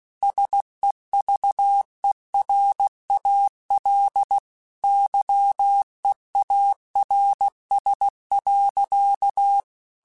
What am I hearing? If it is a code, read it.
Morse code, heard as a keyed tone, about 16 wpm, 800 Hz. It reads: SEVERAL YEARS.